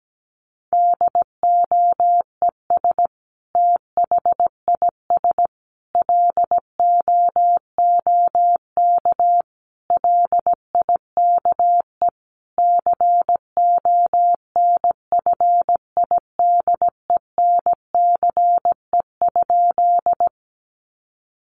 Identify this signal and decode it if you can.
Morse code, heard as a keyed tone, about 17 wpm, 700 Hz. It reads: DOES THIS LOOK LIKE CONFIDENCE?
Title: Sentences from Top 1500 Words 17wpm